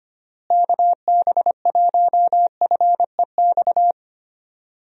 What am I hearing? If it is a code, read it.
Morse code, heard as a keyed tone, about 25 wpm, 700 Hz. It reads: KB1FEX
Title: USA Call Signs - 6 Characters - Repeated 2x - 25wpm